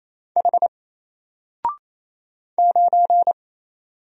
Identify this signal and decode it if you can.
Morse code, heard as a keyed tone, about 27 wpm, 700 Hz. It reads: H 9
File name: Single-Letter-Number-28wpm